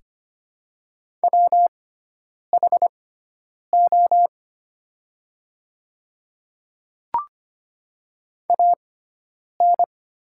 Morse code, heard as a keyed tone, about 25 wpm, spaced as for 9 wpm, 700 Hz. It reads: WHO AN
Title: 6x Character Spacing - Top 100 Words - 25wpm